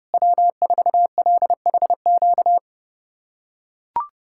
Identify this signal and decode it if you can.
Morse code, heard as a keyed tone, about 32 wpm, 700 Hz. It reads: W4LHQ
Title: USA Callsign DE Callsign - 5 Characters - 30wpm